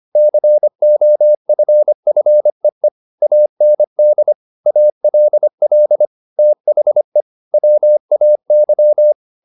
Morse code, heard as a keyed tone, about 25 wpm, 600 Hz. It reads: COFFEE AND ALL THE WAY